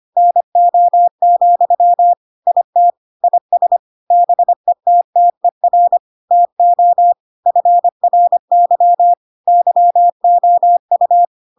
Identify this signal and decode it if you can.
Morse code, heard as a keyed tone, about 25 wpm, 700 Hz. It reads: NO, IT IS BETTER TO FRY YOU
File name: Pinocchio / Pinocchio ch5 25 25 ik1hge